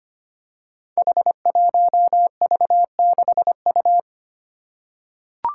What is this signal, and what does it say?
Morse code, heard as a keyed tone, about 25 wpm, 700 Hz. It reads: H1V6U
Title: Five Challenging Letter-Number Combinations 25wpm